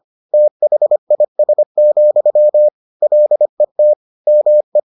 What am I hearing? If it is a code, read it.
Morse code, heard as a keyed tone, about 25 wpm, 600 Hz. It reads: THIS, LET ME